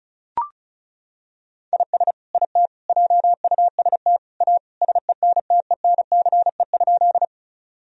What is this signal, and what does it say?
Morse code, heard as a keyed tone, about 34 wpm, 700 Hz. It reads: IS IT JUST A SENTENCE?
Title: Sentences from Top 1000 Words - Review Part 1 - 35wpm